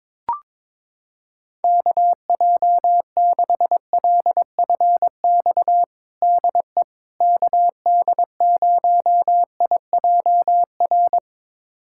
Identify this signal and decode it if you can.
Morse code, heard as a keyed tone, about 22 wpm, 700 Hz. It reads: KJ6LFX DE KD0IJR
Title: USA Callsign DE Callsign - 6 Characters - 22wpm